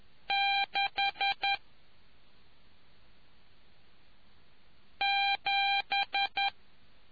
Morse code, heard as a keyed tone, about 10 wpm, 800 Hz. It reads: 6 7